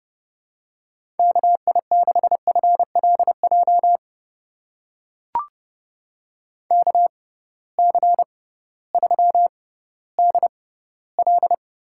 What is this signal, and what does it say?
Morse code, heard as a keyed tone, about 30 wpm, spaced as for 16 wpm, 700 Hz. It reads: KI6FLJ KC3DL
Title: USA Call Signs - 6 Characters - 6x Character Spacing - 30wpm